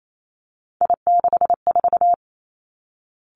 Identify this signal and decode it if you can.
Morse code, heard as a keyed tone, about 28 wpm, 700 Hz. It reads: I64
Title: Three Challenging Letter-Number Combinations 28wpm